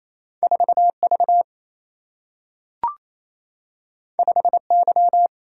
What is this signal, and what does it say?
Morse code, heard as a keyed tone, about 28 wpm, 700 Hz. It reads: 4V 5Y
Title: Single-Letter Single-Number 28wpm